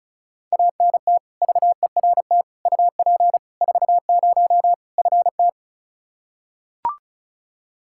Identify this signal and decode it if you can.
Morse code, heard as a keyed tone, about 35 wpm, 700 Hz. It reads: ANT VERT UP 40 FT
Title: Practice QSO - Exchange 2 - Info - 35wpm